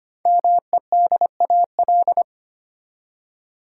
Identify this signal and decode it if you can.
Morse code, heard as a keyed tone, about 25 wpm, 700 Hz. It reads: MEDAL
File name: 5 Letter Words - First 3 Letters Given - 25wpm